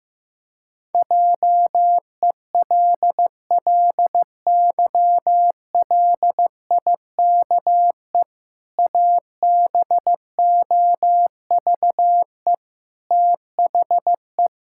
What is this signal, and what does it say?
Morse code, heard as a keyed tone, about 16 wpm, 700 Hz. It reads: JELLYLIKE ABOVE THE
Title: Pangram Challenge v2 - 15wpm